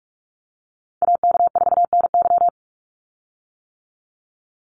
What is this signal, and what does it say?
Morse code, heard as a keyed tone, about 46 wpm, 700 Hz. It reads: AK4NY
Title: Call Signs - 45wpm